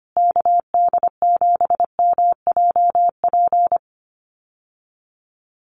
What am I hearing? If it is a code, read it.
Morse code, heard as a keyed tone, about 25 wpm, 700 Hz. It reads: KD7MJP
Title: USA Call Signs Repeated 3x 25wpm